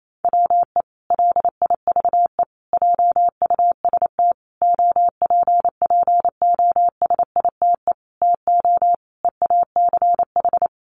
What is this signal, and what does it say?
Morse code, heard as a keyed tone, about 28 wpm, 700 Hz. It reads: WE LIVE JUST OPPOSITE TO EACH
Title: Sentences from Top 1300 Words 28wpm